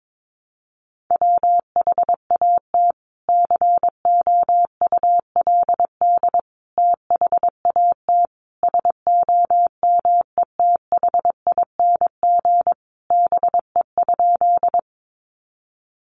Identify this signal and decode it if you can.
Morse code, heard as a keyed tone, about 22 wpm, 700 Hz. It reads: WHAT COULD THAT SOMETHING BE?
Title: Sentences from Top 600 Words 22wpm